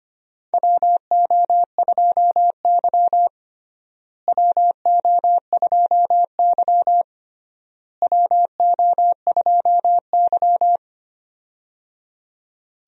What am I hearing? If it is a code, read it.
Morse code, heard as a keyed tone, about 25 wpm, 700 Hz. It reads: WO2Y WO2Y WO2Y
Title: USA Call Signs - 4 Characters - Repeated 3x - 25wpm